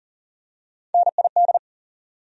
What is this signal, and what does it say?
Morse code, heard as a keyed tone, about 40 wpm, 700 Hz. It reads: NID
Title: USA Call Sign Postfixes - 3 Letters 40wpm